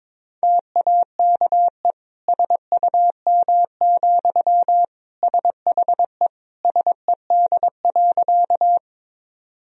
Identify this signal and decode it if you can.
Morse code, heard as a keyed tone, about 22 wpm, 700 Hz. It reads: TAKE SUM, SHE SED.
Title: Sentences from Top 200 Words and Abbreviations 22wpm